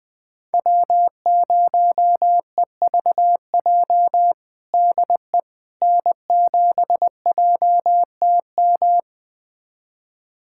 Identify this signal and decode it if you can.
Morse code, heard as a keyed tone, about 20 wpm, 700 Hz. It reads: W0EVJ DE N7JTM
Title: USA Callsign DE Callsign - 5 Characters - 20wpm